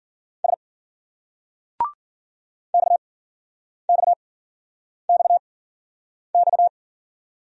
Morse code, heard as a keyed tone, about 47 wpm, spaced as for 9 wpm, 700 Hz. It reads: S XXXX
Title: Speed-Racing Single Letters 40wpm